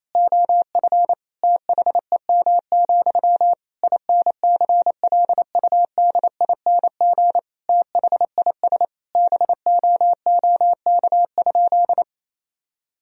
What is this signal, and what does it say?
Morse code, heard as a keyed tone, about 28 wpm, 700 Hz. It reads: OF THEM, INCLUDING THIS BOOK?
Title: Sentences from Top 1000 Words - Review Part 4 - 28wpm